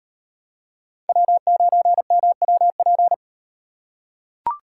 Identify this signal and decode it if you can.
Morse code, heard as a keyed tone, about 38 wpm, 700 Hz. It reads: W9MWP